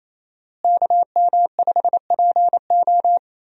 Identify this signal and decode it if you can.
Morse code, heard as a keyed tone, about 28 wpm, 700 Hz. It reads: KM5PO
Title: USA Call Signs - 5 Characters - Repeated 2x - 28wpm